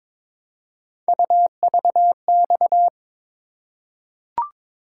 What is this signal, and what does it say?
Morse code, heard as a keyed tone, about 23 wpm, 700 Hz. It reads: UVX
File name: practice / 22wpm / Three Letter-Number Combinations - 22wpm